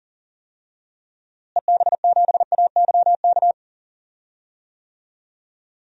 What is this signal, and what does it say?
Morse code, heard as a keyed tone, about 40 wpm, 700 Hz. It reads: EB7AYK
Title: DXCC-100 Call Signs - 6 Characters - Repeated 3x - 40wpm